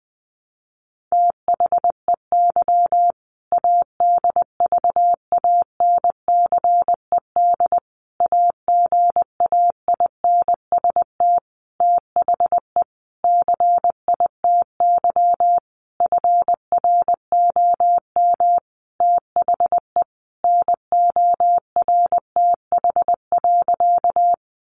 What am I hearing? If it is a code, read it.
Morse code, heard as a keyed tone, about 19 wpm, 700 Hz. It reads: THEY ADVANCED AGAINST THE CITY FROM THE NORTH.
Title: Sentences from Top 1200 Words 20wpm